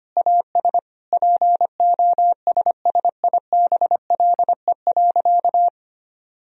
Morse code, heard as a keyed tone, about 25 wpm, 700 Hz. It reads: AS POSSIBLE.
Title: Sentences from Top 1000 Words - Review Part 1 - 25wpm